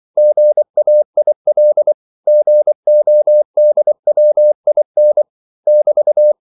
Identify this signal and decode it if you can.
Morse code, heard as a keyed tone, about 24 wpm, 600 Hz. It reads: GAIL GODWIN <BT>
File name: CW / Practice / EnglishQuotations0324